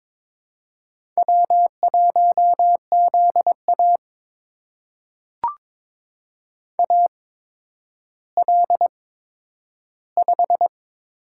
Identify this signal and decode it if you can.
Morse code, heard as a keyed tone, about 22 wpm, spaced as for 11 wpm, 700 Hz. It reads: W1ZA AL5
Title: USA Call Signs - 4 Characters - 8x Character Spacing - 22wpm